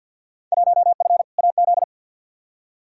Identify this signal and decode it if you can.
Morse code, heard as a keyed tone, about 54 wpm, 700 Hz. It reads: 1F AZ
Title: Field Day Exchanges - 50wpm